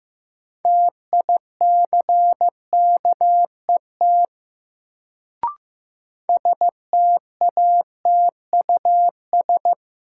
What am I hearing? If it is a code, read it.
Morse code, heard as a keyed tone, about 15 wpm, 700 Hz. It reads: TICKET STATUS
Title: Sentences from Top 1400 Words - New Word Drill - 15wpm